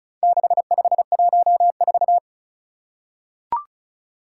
Morse code, heard as a keyed tone, about 36 wpm, 700 Hz. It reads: 6514
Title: Four Challenging Letter-Number Combinations 35wpm